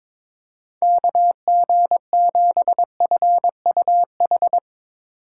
Morse code, heard as a keyed tone, about 22 wpm, 700 Hz. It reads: KG7FUH
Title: USA Call Signs - 6 Characters - 8x Character Spacing - 22wpm